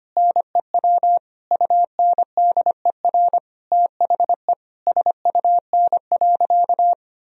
Morse code, heard as a keyed tone, about 25 wpm, 700 Hz. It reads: NEW UNDER THE SUN.
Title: Sentences from Top 1000 Words - Review Part 1 - 25wpm